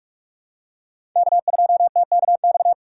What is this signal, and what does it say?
Morse code, heard as a keyed tone, about 46 wpm, 700 Hz. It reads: K2TKX
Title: USA Call Signs - 5 Characters - 3x Character Spacing - 45wpm